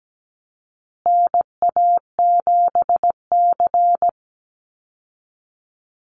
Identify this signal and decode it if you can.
Morse code, heard as a keyed tone, about 17 wpm, 700 Hz. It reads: NA7C
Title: Call Signs Repeated 2x - 17wpm